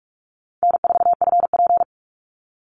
Morse code, heard as a keyed tone, about 44 wpm, 700 Hz. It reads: N4FP
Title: Call Signs Repeated 2x - 45wpm